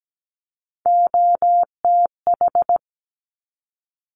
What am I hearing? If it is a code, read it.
Morse code, heard as a keyed tone, about 17 wpm, 700 Hz. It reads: OTH
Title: Top Three Letter Combinations in Words 17wpm